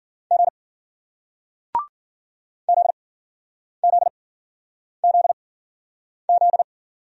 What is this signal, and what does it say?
Morse code, heard as a keyed tone, about 55 wpm, 700 Hz. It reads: G Z Z Z Z